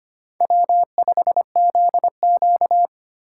Morse code, heard as a keyed tone, about 25 wpm, 700 Hz. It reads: W5ZQ
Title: USA Call Signs - 4 Characters - 2x Character Spacing - 25wpm